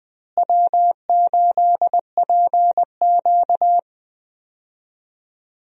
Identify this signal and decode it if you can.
Morse code, heard as a keyed tone, about 20 wpm, 700 Hz. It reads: W8PQ
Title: USA Call Signs - 4 Characters - Repeated 2x - 20wpm